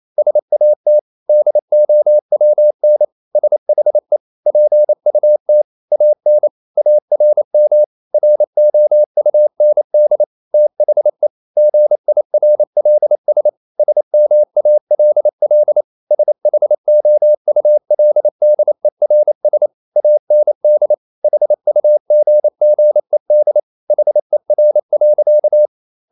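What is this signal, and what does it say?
Morse code, heard as a keyed tone, about 28 wpm, 600 Hz. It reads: SAT DOWN SHE PUT AN ARM ROUND THE GIRLS SMALL SHOULDERS AND HUGGED HER.
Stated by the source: Morse code practice at 28 wpm, text of Ken Follett's novel Never in English, from CW med Gnister.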